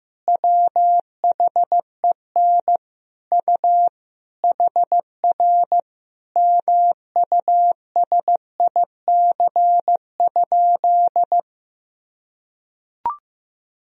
Morse code, heard as a keyed tone, about 16 wpm, 700 Hz. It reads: WHEN U HR MUSIC?
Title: Sentences from Top 300 Words and Abbreviations 15wpm